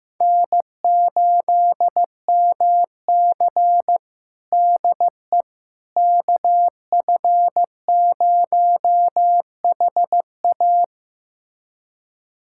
Morse code, heard as a keyed tone, about 15 wpm, 700 Hz. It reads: N8MC DE KF0HA
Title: USA Callsign DE Callsign 15wpm